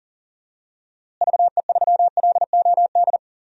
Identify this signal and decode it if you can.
Morse code, heard as a keyed tone, about 37 wpm, 700 Hz. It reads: VE3LOD
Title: DXCC-100 Call Signs - 6 Characters - Repeated 2x - 40wpm